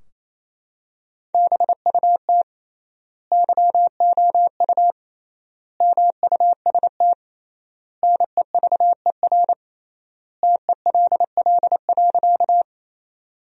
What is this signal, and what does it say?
Morse code, heard as a keyed tone, about 28 wpm, 700 Hz. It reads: BUT YOU MUST NEVER TELL.
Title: 3x Word-Spacing - Sentences from Top 300 Words - 28wpm